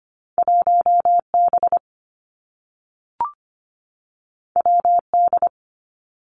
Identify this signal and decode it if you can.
Morse code, heard as a keyed tone, about 25 wpm, 700 Hz. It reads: 1B WD
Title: Two Challenging Letter-Number Combinations 25wpm